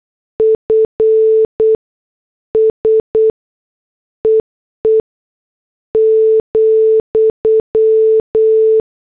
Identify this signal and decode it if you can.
Morse code, heard as a keyed tone, about 8 wpm, spaced as for 6 wpm, 440 Hz. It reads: FSEE,
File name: morse / headlines